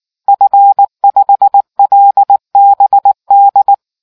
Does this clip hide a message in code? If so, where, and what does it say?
Yes: Morse code from 0.3 to 3.7 s, heard as a keyed tone, about 19 wpm, 800 Hz. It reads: F5LBD